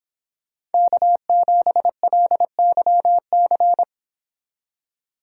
Morse code, heard as a keyed tone, about 30 wpm, 700 Hz. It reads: K7LYC